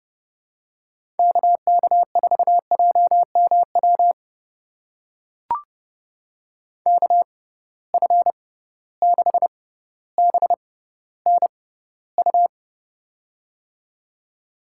Morse code, heard as a keyed tone, about 30 wpm, spaced as for 15 wpm, 700 Hz. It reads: KK4JMW KF6BNU
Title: USA Call Signs - 6 Characters - 6x Character Spacing - 30wpm